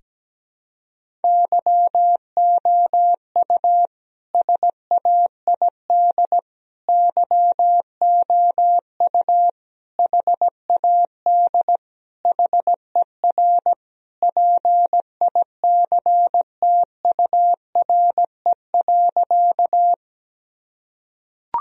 Morse code, heard as a keyed tone, about 17 wpm, 700 Hz. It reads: YOU SAID YOU HAD HER PICTURE.
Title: Sentences from Top 1000 Words - Review Part 1 - 17wpm